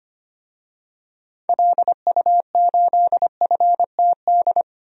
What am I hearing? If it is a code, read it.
Morse code, heard as a keyed tone, about 25 wpm, 700 Hz. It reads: LU8FTD
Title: DXCC-100 Call Signs - 6 Characters - Repeated 2x - 25wpm